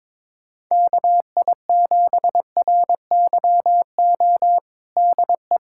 Morse code, heard as a keyed tone, about 22 wpm, 700 Hz. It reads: KI7RYO DE
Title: USA Callsign DE Callsign - 6 Characters - 22wpm